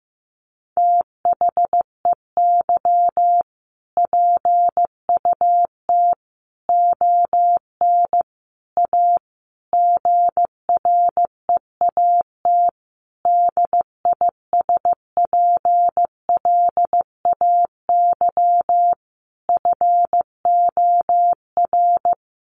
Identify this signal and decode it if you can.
Morse code, heard as a keyed tone, about 15 wpm, 700 Hz. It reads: THEY PUT ON A GREAT DISPLAY FOR
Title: Sentences from Top 1200 Words 15wpm